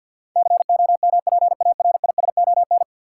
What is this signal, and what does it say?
Morse code, heard as a keyed tone, about 53 wpm, 700 Hz. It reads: COMPARISON
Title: Sentences from Top 1600 Words - New Word Drill Alt - 50wpm